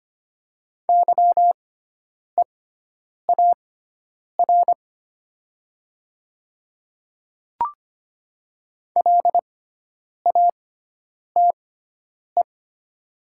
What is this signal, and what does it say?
Morse code, heard as a keyed tone, about 25 wpm, spaced as for 9 wpm, 700 Hz. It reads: YEAR LATE
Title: 6x Character Spacing - Top 500 Words - 25wpm